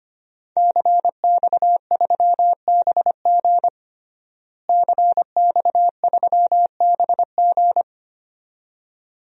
Morse code, heard as a keyed tone, about 25 wpm, 700 Hz. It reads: CX3BG CX3BG
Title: DXCC-100 Call Signs - 5 Characters - Repeated 2x - 25wpm